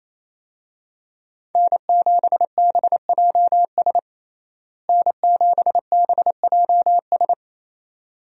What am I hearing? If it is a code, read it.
Morse code, heard as a keyed tone, about 28 wpm, 700 Hz. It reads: N7BJS N7BJS
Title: USA Call Signs - 5 Characters - Repeated 2x - 28wpm